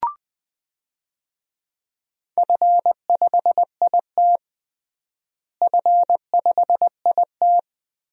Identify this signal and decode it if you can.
Morse code, heard as a keyed tone, about 20 wpm, 700 Hz. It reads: F5IT F5IT
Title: DXCC-100 Call Signs - 4 Characters - Repeated 2x - 20wpm